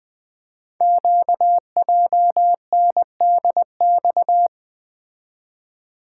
Morse code, heard as a keyed tone, about 20 wpm, 700 Hz. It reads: QJNDX
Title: Code Groups - 5 Letters - 20wpm